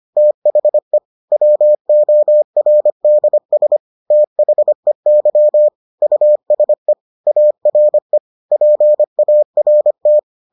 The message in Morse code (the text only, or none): THE WORDS THEY USE ARE PART